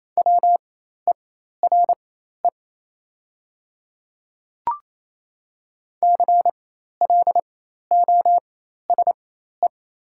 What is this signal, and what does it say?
Morse code, heard as a keyed tone, about 27 wpm, spaced as for 13 wpm, 690 Hz. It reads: WERE CLOSE
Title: 4x Character Spacing - Top 300 Words - 28wpm